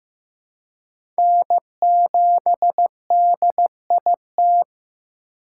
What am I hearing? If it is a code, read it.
Morse code, heard as a keyed tone, about 15 wpm, 700 Hz. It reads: N7DIT
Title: USA Call Signs - 5 Characters - 8x Character Spacing - 15wpm